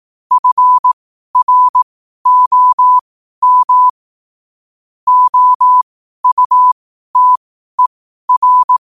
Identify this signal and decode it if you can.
Morse code, heard as a keyed tone, about 18 wpm, spaced as for 12 wpm, 1000 Hz. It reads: FROM OUTER
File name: qotdfiles / 12wpm